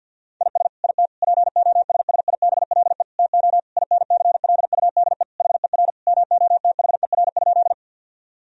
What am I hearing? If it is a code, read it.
Morse code, heard as a keyed tone, about 52 wpm, 700 Hz. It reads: IS IT POSSIBLE TO INCLUDE HER MOTHER?
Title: Sentences from Top 1000 Words - Review Part 3 - 50wpm